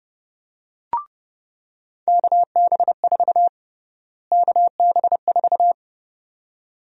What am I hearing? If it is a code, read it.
Morse code, heard as a keyed tone, about 32 wpm, 700 Hz. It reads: KB4 KB4
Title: USA Call Sign Prefixes - Repeated 2x - 30wpm